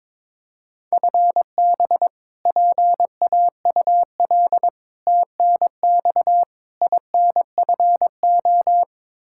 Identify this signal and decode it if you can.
Morse code, heard as a keyed tone, about 22 wpm, 700 Hz. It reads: FB PAUL TNX INFO
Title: Practice QSO - Exchange 3 - More Info - 22wpm